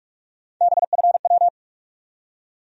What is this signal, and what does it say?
Morse code, heard as a keyed tone, about 46 wpm, 700 Hz. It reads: BFW
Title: USA Call Sign Postfixes - 3 Letters 45wpm